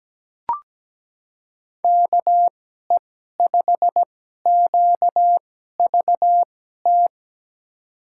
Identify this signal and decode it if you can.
Morse code, heard as a keyed tone, about 17 wpm, spaced as for 12 wpm, 700 Hz. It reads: KE5QVT
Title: USA Call Signs - 6 Characters - 2x Character Spacing - 17wpm